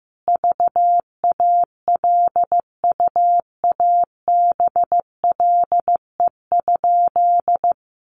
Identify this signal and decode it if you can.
Morse code, heard as a keyed tone, about 15 wpm, 700 Hz. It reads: VALUABLE?
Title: Sentences from Top 1100 Words 15wpm